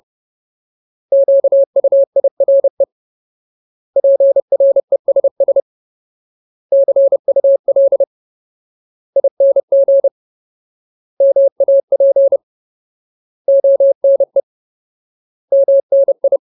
Morse code, heard as a keyed tone, about 30 wpm, 550 Hz. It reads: QUIRE PRESS CUL ING MAP ONE MNI
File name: Ultimate Word List / 30WPM Ultimate